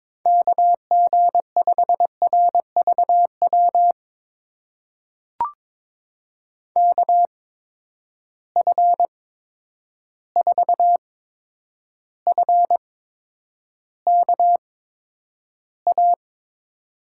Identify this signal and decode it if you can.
Morse code, heard as a keyed tone, about 22 wpm, spaced as for 9 wpm, 700 Hz. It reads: KG5RVW KF4FKA
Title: USA Call Signs - 6 Characters - 8x Character Spacing - 22wpm